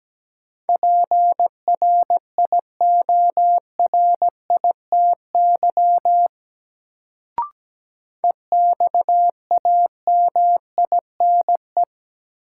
Morse code, heard as a keyed tone, about 17 wpm, 700 Hz. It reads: PRIORITY EXAMINE
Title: Sentences from Top 1600 Words - New Word Drill - 17wpm